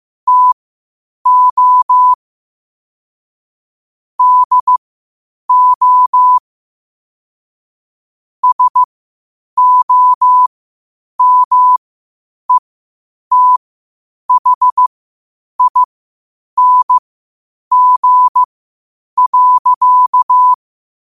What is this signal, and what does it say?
Morse code, heard as a keyed tone, about 15 wpm, spaced as for 8 wpm, 1000 Hz. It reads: TO DO SOMETHING.
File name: qotdfiles / 8wpm